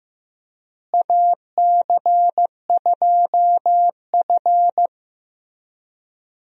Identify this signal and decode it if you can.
Morse code, heard as a keyed tone, about 15 wpm, 700 Hz. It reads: AC2F
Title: USA Call Signs - 4 Characters - Repeated 3x - 15wpm